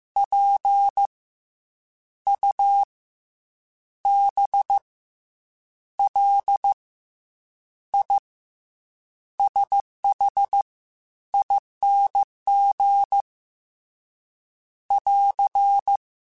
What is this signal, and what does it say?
Morse code, heard as a keyed tone, about 15 wpm, 800 Hz. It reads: P U B L I SH ING <AR>
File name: Special INGs / Special INGs-mp3s / SlowToFaster ing @13